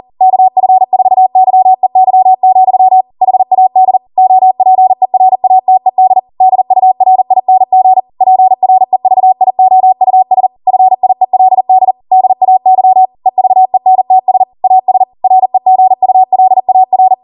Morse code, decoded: KF4YEY, HAD OPERATED DURING PREVIOUS FIELD DAY EVENTS AS REGULAR